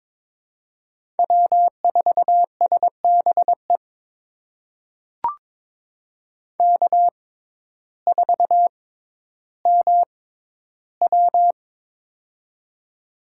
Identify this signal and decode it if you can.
Morse code, heard as a keyed tone, about 22 wpm, spaced as for 12 wpm, 700 Hz. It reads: W4SBE K4MW
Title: USA Call Signs - 6x Character Spacing - 22wpm